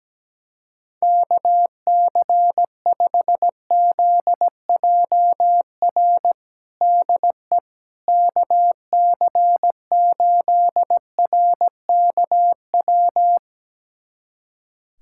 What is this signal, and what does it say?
Morse code, heard as a keyed tone, about 17 wpm, 700 Hz. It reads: KC5ZJR DE KC8RKW
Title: USA Callsign DE Callsign - 6 Characters - 17wpm